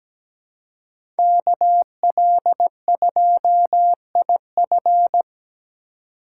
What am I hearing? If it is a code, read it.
Morse code, heard as a keyed tone, about 17 wpm, 700 Hz. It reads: KL2IF
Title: USA Call Signs - 5 Characters - 3x Character Spacing - 17wpm